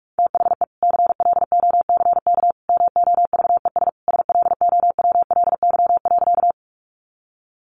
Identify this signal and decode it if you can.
Morse code, heard as a keyed tone, about 44 wpm, 700 Hz. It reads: THE CLOCK MOVES SLOWLY.
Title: Easy Phrases 45wpm